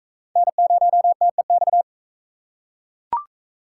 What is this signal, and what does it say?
Morse code, heard as a keyed tone, about 41 wpm, 700 Hz. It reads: N0TEX